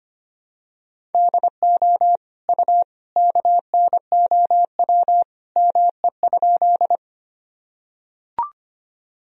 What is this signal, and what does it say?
Morse code, heard as a keyed tone, about 25 wpm, 700 Hz. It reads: DO U KNOW ME?
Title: Sentences from Top 200 Words and Abbreviations 25wpm